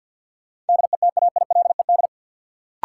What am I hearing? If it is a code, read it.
Morse code, heard as a keyed tone, about 47 wpm, 700 Hz. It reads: DETAILED
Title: Sentences from Top 1500 Words - New Word Drill Alt - 50wpm